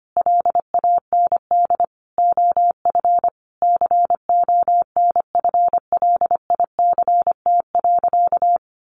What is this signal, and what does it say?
Morse code, heard as a keyed tone, about 25 wpm, 700 Hz. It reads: LAND OF CONFLICT.
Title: Sentences from Top 1300 Words 25wpm